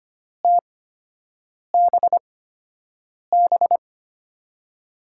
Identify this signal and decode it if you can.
Morse code, heard as a keyed tone, about 25 wpm, spaced as for 7 wpm, 700 Hz. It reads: TBB
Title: USA Call Signs - 5 Characters - 8x Character Spacing - 25wpm